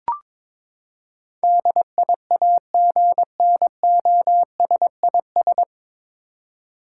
Morse code, heard as a keyed tone, about 22 wpm, 700 Hz. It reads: DIAGNOSIS
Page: Morse Code Ninja practice 22wpm